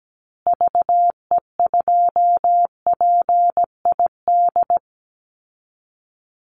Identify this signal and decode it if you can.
Morse code, heard as a keyed tone, about 17 wpm, 700 Hz. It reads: VE2PID
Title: Call Signs Repeated 3x - 17wpm